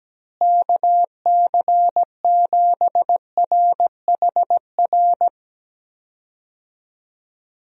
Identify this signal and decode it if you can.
Morse code, heard as a keyed tone, about 17 wpm, 700 Hz. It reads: KC7RHR
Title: USA Call Signs - 6 Characters - Repeated 3x - 17wpm